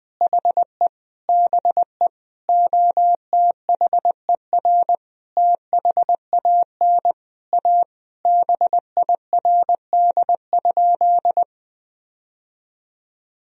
Morse code, heard as a keyed tone, about 20 wpm, 700 Hz. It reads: HE BE OTHER THAN A BIRD?
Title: Sentences from Top 1000 Words - Review Part 2 - 20wpm